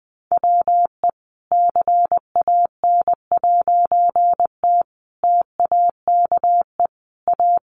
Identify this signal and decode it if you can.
Morse code, heard as a keyed tone, about 20 wpm, 700 Hz. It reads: WE CAN'T TAKE A
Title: A Sound Of Thunder 20wpm